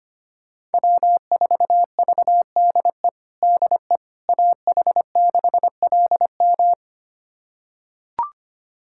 Morse code, heard as a keyed tone, about 25 wpm, 700 Hz. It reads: W4VDE DE AH6LM
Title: USA Callsign DE Callsign - 5 Characters - 25wpm